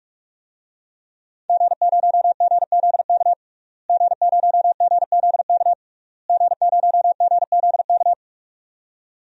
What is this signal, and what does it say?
Morse code, heard as a keyed tone, about 46 wpm, 700 Hz. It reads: G0GZK G0GZK G0GZK
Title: DXCC-100 Call Signs - 5 Characters - Repeated 3x - 45wpm